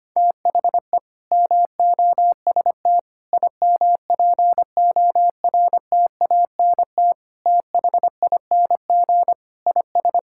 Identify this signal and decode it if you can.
Morse code, heard as a keyed tone, about 25 wpm, 700 Hz. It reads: THE MOST IMPORTANT THING IS
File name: practice / 25wpm / Sentences with 3-Word Phrases - 25wpm